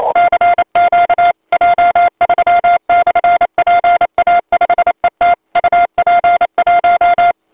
Morse code, heard as a keyed tone, about 29 wpm, 1400 Hz, with noise. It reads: CQ J3/PA5ET UP1
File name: voice / 80m j3 pa5et 002 2001 08 08